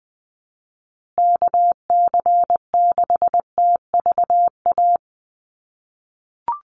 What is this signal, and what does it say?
Morse code, heard as a keyed tone, about 20 wpm, 700 Hz. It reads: KC6TVA
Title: USA Call Signs Repeated 3x 20wpm